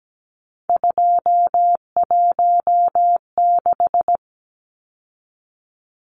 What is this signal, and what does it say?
Morse code, heard as a keyed tone, about 17 wpm, 700 Hz. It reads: 216
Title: Numbers - 3-Digit 17wpm